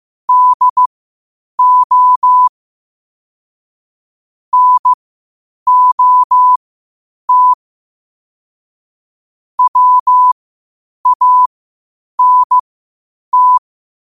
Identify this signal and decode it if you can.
Morse code, heard as a keyed tone, about 15 wpm, spaced as for 8 wpm, 1000 Hz. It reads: DO NOT WANT
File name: qotdfiles / 8wpm